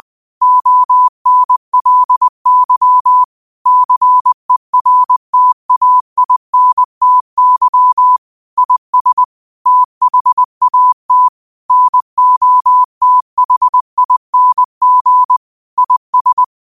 Morse code, heard as a keyed tone, about 20 wpm, 1000 Hz. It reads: ONLY CERTAINTY IS THAT NOTHING IS
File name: qotdfiles / 20wpm